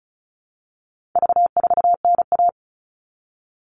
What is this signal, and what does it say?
Morse code, heard as a keyed tone, about 34 wpm, 700 Hz. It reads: V4NA
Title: All Cut Numbers - 4-Digit 35wpm